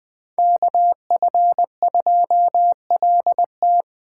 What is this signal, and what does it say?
Morse code, heard as a keyed tone, about 20 wpm, 700 Hz. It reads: KF2LT